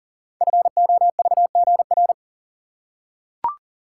Morse code, heard as a keyed tone, about 42 wpm, 690 Hz. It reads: FOVGR
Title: Code Groups - 5 Letters - 40wpm